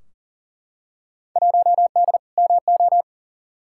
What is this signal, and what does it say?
Morse code, heard as a keyed tone, about 40 wpm, 700 Hz. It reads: 1D MO
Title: Field Day Exchanges - 40wpm